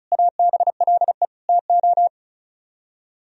Morse code, heard as a keyed tone, about 44 wpm, 700 Hz. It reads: ABLE TO